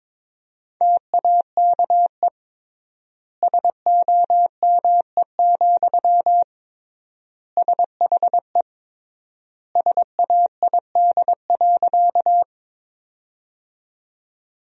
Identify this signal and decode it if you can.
Morse code, heard as a keyed tone, about 22 wpm, 700 Hz. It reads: TAKE SOME, SHE SAID.
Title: 3x Word-Spacing - Sentences from Top 200 Words - 22wpm